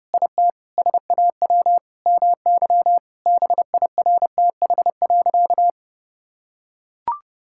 Morse code, heard as a keyed tone, about 32 wpm, 700 Hz. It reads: IT SAW MY BIRTH.
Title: Sentences from Top 1000 Words - Review Part 5 - 30wpm